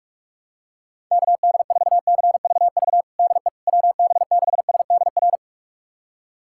Morse code, heard as a keyed tone, about 43 wpm, 700 Hz. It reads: KD4CVV DE WB6SDR
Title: USA Callsign DE Callsign - 6 Characters - 45wpm